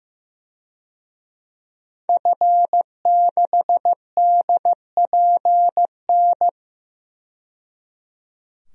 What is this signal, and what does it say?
Morse code, heard as a keyed tone, about 15 wpm, 700 Hz. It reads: F6DPN
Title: DXCC-100 Call Signs - 5 Characters - 15wpm